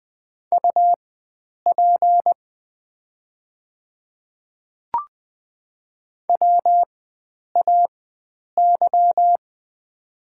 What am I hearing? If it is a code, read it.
Morse code, heard as a keyed tone, about 20 wpm, spaced as for 9 wpm, 700 Hz. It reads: UP WAY
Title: 4x Character Spacing - Top 300 Words - 20wpm